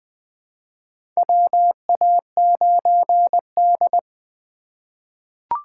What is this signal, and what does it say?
Morse code, heard as a keyed tone, about 20 wpm, 700 Hz. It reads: WA9D
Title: USA Call Signs - 3x Character Spacing - 20wpm